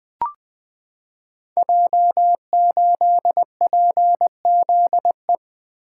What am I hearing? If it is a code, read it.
Morse code, heard as a keyed tone, about 20 wpm, 700 Hz. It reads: J8PZE
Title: Code Groups - 5 Letter:Numbers - 20wpm